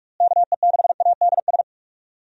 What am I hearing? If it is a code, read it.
Morse code, heard as a keyed tone, about 44 wpm, 700 Hz. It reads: KE6ADS
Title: USA Call Signs - 6 Characters - Repeated 2x - 45wpm